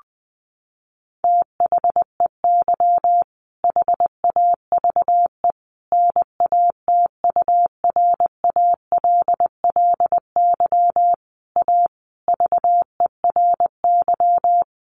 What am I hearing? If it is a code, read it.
Morse code, heard as a keyed tone, about 20 wpm, 700 Hz. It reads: THEY HAVE NATURALLY A VERY
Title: Sentences from Top 1300 Words 20wpm